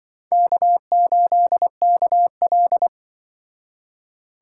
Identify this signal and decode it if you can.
Morse code, heard as a keyed tone, about 25 wpm, 700 Hz. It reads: K8KL
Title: Speed Racing - USA 1x2 Call Signs 20wpm